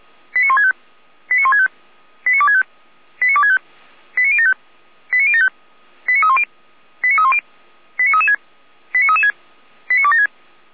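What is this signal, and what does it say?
Five-tone selective call (ZVEI), heard as quick runs of five tones, tones of about 70 ms. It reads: 89165 89165 89265 89265 89075 89075 89310 89310 89336 89336 89276
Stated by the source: Five-tone ZVEI selective call received by SDR, from a server of Feuerwehr Orlamünde (fire brigade).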